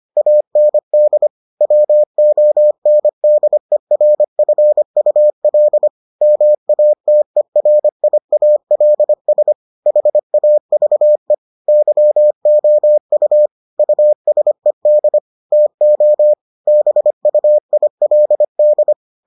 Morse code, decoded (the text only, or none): AND WONDERFUL MATERIALS HAVE YOU USED TO BUILD